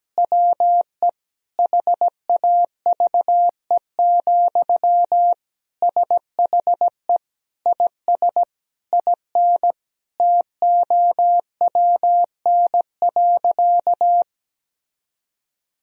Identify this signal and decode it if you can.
Morse code, heard as a keyed tone, about 17 wpm, 700 Hz. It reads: WE HAVE, SHE IS IN TOWN.
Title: Sentences from Top 1000 Words - Review Part 4 - 17wpm